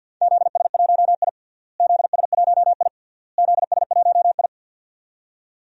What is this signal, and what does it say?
Morse code, heard as a keyed tone, about 51 wpm, 700 Hz. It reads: ZS1I ZS1I ZS1I
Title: DXCC-100 Call Signs - 4 Characters - Repeated 3x - 50wpm